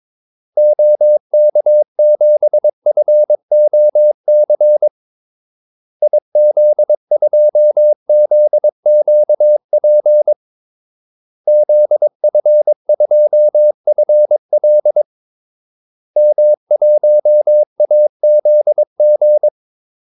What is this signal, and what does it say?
Morse code, heard as a keyed tone, about 22 wpm, 600 Hz. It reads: OK7FOC IZ2ZQP ZF2FL M1AZG
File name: Callsigns-22wpm